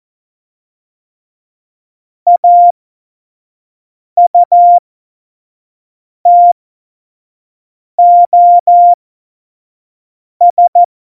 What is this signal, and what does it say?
Morse code, heard as a keyed tone, about 14 wpm, spaced as for 5 wpm, 700 Hz. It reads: AUTOS
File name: Morsekurs / Witze / Witz 19. Februar 2019, 25 F70